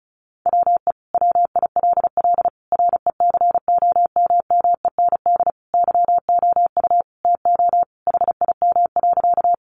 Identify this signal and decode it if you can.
Morse code, heard as a keyed tone, about 34 wpm, 700 Hz. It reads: WE WILL RECOMMEND YOU TO HIM.
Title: Sentences from Top 1200 Words 35wpm